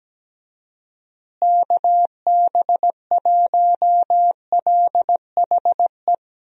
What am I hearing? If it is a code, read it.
Morse code, heard as a keyed tone, about 17 wpm, 700 Hz. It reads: KB1LHE
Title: USA Call Signs - 6 Characters - Repeated 2x - 17wpm